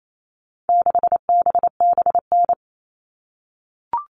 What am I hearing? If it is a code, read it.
Morse code, heard as a keyed tone, about 28 wpm, 700 Hz. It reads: 6BBN
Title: All Cut Numbers - 4-Digit 28wpm